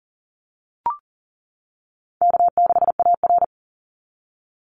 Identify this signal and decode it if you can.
Morse code, heard as a keyed tone, about 40 wpm, 700 Hz. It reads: K6AR
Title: Easy Call Signs - 40wpm